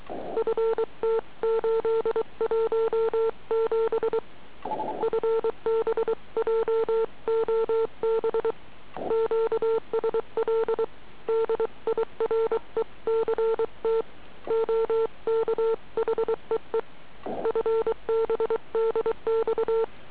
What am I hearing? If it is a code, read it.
Morse code, heard as a keyed tone, about 23 wpm, 450 Hz. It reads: FT817 FB JOB QSL DIRECT OK HEE FB DX